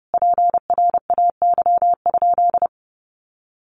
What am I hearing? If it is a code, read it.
Morse code, heard as a keyed tone, about 30 wpm, 700 Hz. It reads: PRAY?
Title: A Sound Of Thunder 30wpm